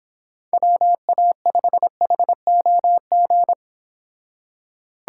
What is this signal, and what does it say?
Morse code, heard as a keyed tone, about 25 wpm, 700 Hz. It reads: WA5HOG